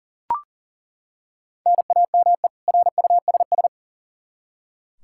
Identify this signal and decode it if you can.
Morse code, heard as a keyed tone, about 40 wpm, 700 Hz. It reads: NAME RUSS
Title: K1USN SST QSO Practice 40wpm